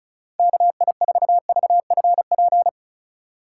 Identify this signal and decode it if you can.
Morse code, heard as a keyed tone, about 36 wpm, 700 Hz. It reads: KI4VFP
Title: USA Call Signs - 6 Characters - 3x Character Spacing - 35wpm